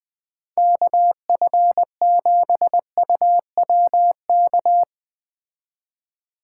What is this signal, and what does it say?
Morse code, heard as a keyed tone, about 19 wpm, 700 Hz. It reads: KF7UWK
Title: Speed Racing - USA 2x3 Call Signs 17wpm